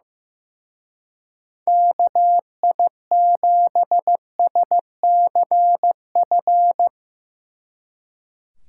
Morse code, heard as a keyed tone, about 16 wpm, 700 Hz. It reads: KI7SCF
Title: Speed Racing - USA Call Signs 15wpm